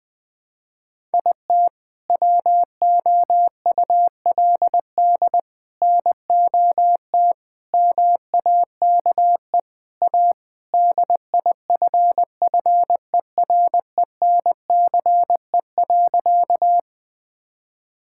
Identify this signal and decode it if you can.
Morse code, heard as a keyed tone, about 21 wpm, 700 Hz. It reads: IT WOULD NOT MAKE A DIFFERENCE.
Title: Sentences from Top 1000 Words - Review Part 5 - 20wpm